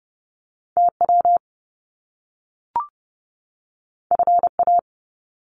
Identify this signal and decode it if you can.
Morse code, heard as a keyed tone, about 30 wpm, 700 Hz. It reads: TW FA
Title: Top Two Letter Combinations in Words 30wpm